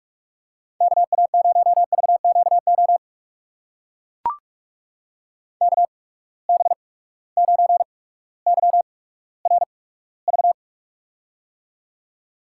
Morse code, heard as a keyed tone, about 42 wpm, spaced as for 19 wpm, 700 Hz. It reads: KA0VQO KB9YRV
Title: USA Call Signs - 6 Characters - 8x Character Spacing - 45wpm